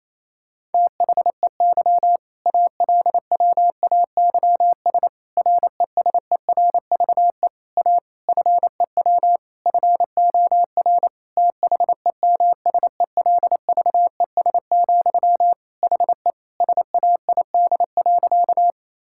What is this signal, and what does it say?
Morse code, heard as a keyed tone, about 27 wpm, 690 Hz. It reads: THEY ALWAYS RESERVE A FEW FOR THEMSELVES, HE SAID.
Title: Sentences from Top 1000 Words - Review Part 6 - 28wpm